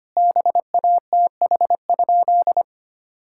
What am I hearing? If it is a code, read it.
Morse code, heard as a keyed tone, about 25 wpm, 700 Hz. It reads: BATH?
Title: Sentences from Top 1600 Words 25wpm